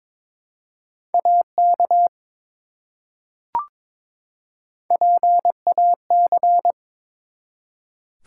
Morse code, heard as a keyed tone, about 22 wpm, 700 Hz. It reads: AK PAC
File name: ARRL Sections v2 - 22wpm